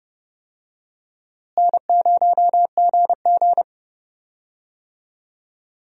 Morse code, heard as a keyed tone, about 30 wpm, 700 Hz. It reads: N0GG
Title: USA Call Signs - 4 Characters - Repeated 3x - 30wpm